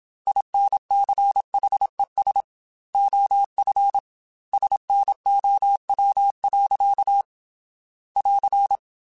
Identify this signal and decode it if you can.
Morse code, heard as a keyed tone, about 26 wpm, 800 Hz. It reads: INCHES OF SNOW. <AR>
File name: Weather Sets / Weather Sets-mp3s / Weather 3 17 @25